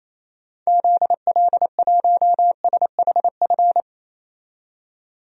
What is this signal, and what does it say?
Morse code, heard as a keyed tone, about 28 wpm, 700 Hz. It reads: ZL1SHF
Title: DXCC-100 Call Signs - 6 Characters - 28wpm